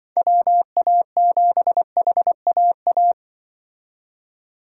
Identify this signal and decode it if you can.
Morse code, heard as a keyed tone, about 24 wpm, 700 Hz. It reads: WA7HAA
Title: Speed Racing - USA 2x3 Call Signs 20wpm